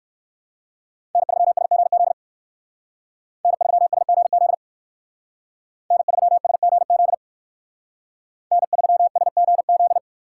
Morse code, heard as a keyed tone, about 58 wpm, 700 Hz. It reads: N3SGZ N3SGZ N3SGZ N3SGZ